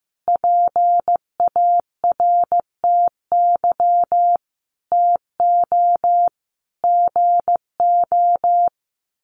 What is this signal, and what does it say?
Morse code, heard as a keyed tone, about 15 wpm, 700 Hz. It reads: PARTY TO GO
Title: Sentences from Top 1400 Words 15wpm